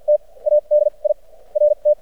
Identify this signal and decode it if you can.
Morse code, heard as a keyed tone, about 56 wpm, spaced as for 26 wpm, 600 Hz. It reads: TA DIRT
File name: archived / CRSnet